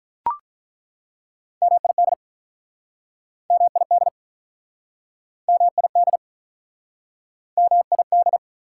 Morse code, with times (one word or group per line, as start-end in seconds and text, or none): 1.6-2.1 MID
3.5-4.1 MID
5.5-6.2 MID
7.6-8.4 MID